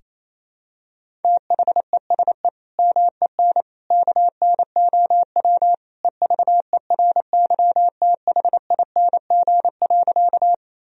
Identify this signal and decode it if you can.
Morse code, heard as a keyed tone, about 28 wpm, 700 Hz. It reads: THESE MEN KNOW EVERYTHING.
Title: Sentences from Top 1000 Words - Review Part 1 - 28wpm